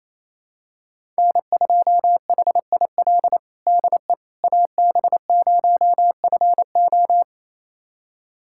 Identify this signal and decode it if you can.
Morse code, heard as a keyed tone, about 28 wpm, 700 Hz. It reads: N2HIL DE AB0FO
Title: USA Callsign DE Callsign - 5 Characters - 28wpm